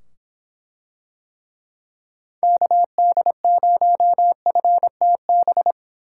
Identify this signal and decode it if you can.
Morse code, heard as a keyed tone, about 24 wpm, 700 Hz. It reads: KD0FTB